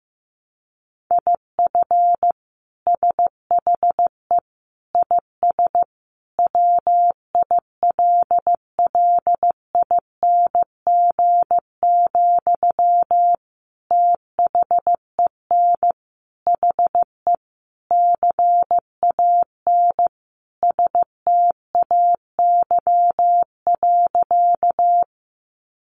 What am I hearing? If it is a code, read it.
Morse code, heard as a keyed tone, about 15 wpm, 700 Hz. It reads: IF SHE IS WILLING, THEN HE CAN STAY.
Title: Sentences from Top 1000 Words 15wpm